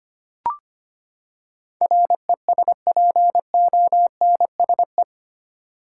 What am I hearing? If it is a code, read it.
Morse code, heard as a keyed tone, about 25 wpm, 700 Hz. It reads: RESPONSE
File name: Eight Letter Words - 25wpm